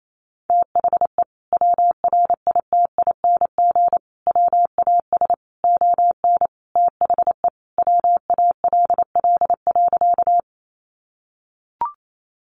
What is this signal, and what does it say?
Morse code, heard as a keyed tone, about 28 wpm, 700 Hz. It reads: THE WRITING WAS ON THE WALL.
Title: Sentences from Top 1100 Words 28wpm